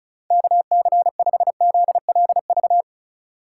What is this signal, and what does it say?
Morse code, heard as a keyed tone, about 36 wpm, 700 Hz. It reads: KC5ZLV
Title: USA Call Signs - 6 Characters - 3x Character Spacing - 35wpm